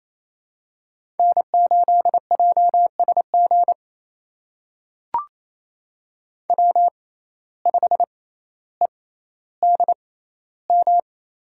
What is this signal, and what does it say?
Morse code, heard as a keyed tone, about 27 wpm, spaced as for 14 wpm, 700 Hz. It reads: N8JSG W5EDM
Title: USA Call Signs - 5 Characters - 6x Character Spacing - 28wpm